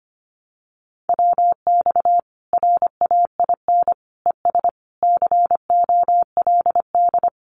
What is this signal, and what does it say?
Morse code, heard as a keyed tone, about 25 wpm, 700 Hz. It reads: WX RAIN ES COLD
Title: Common QSO Phrases 25wpm